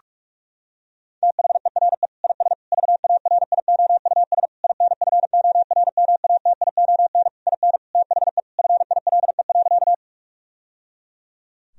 Morse code, heard as a keyed tone, about 46 wpm, 700 Hz. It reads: THERE IS VARIOUS INFORMATION IN THE FILE.
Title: Sentences from Top 1000 Words - Review Part 4 - 45wpm